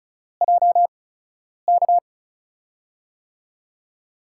Morse code, heard as a keyed tone, about 36 wpm, 700 Hz. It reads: J K
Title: USA Call Signs - 6 Characters - 8x Character Spacing - 35wpm